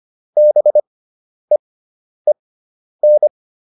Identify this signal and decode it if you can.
Morse code, heard as a keyed tone, about 25 wpm, spaced as for 10 wpm, 600 Hz. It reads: BEEN
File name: QSO310 10